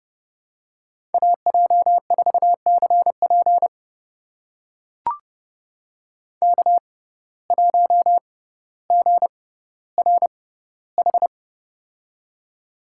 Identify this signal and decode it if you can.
Morse code, heard as a keyed tone, about 30 wpm, spaced as for 15 wpm, 700 Hz. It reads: AJ4CP K1GRH
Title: USA Call Signs - 5 Characters - 6x Character Spacing - 30wpm